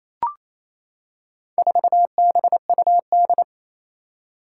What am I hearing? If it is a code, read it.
Morse code, heard as a keyed tone, about 28 wpm, 700 Hz. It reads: 4BUD
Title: Four Challenging Letter-Number Combinations 28wpm